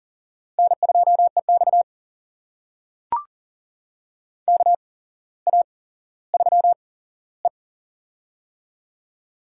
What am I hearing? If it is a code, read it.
Morse code, heard as a keyed tone, about 40 wpm, spaced as for 17 wpm, 700 Hz. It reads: N2EX KA3E
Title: USA Call Signs - 4 Characters - 8x Character Spacing - 40wpm